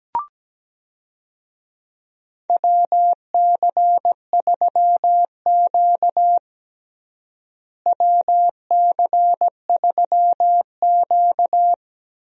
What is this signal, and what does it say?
Morse code, heard as a keyed tone, about 17 wpm, 700 Hz. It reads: WC3Q WC3Q
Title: USA Call Signs - 4 Characters - Repeated 2x - 17wpm